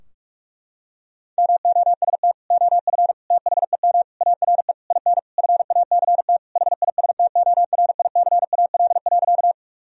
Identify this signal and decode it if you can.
Morse code, heard as a keyed tone, about 46 wpm, 700 Hz. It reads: MOST OF THEM ARE IN FACT HISTORICAL.
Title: Sentences from Top 1000 Words - Review Part 4 - 45wpm